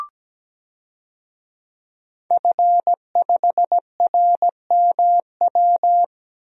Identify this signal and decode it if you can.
Morse code, heard as a keyed tone, about 17 wpm, 700 Hz. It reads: F5RMW
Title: DXCC-100 Call Signs - 5 Characters - Repeated 3x - 17wpm